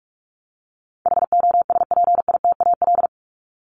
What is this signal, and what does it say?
Morse code, heard as a keyed tone, about 44 wpm, 700 Hz. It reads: HOSPITAL
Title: Top 1751-2000 Words 45wpm